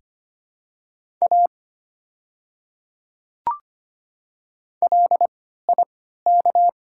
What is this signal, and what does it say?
Morse code, heard as a keyed tone, about 25 wpm, 700 Hz. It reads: A L I K
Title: 3x Character Spacing - Top 100 Words - 25wpm